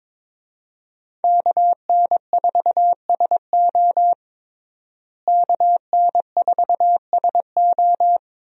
Morse code, heard as a keyed tone, about 21 wpm, 700 Hz. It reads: KN4SO KN4SO
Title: USA Call Signs - 5 Characters - Repeated 3x - 22wpm